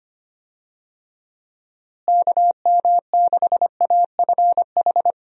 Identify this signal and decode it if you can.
Morse code, heard as a keyed tone, about 25 wpm, 700 Hz. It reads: KM6AFH
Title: USA Call Signs - 6 Characters - Repeated 2x - 25wpm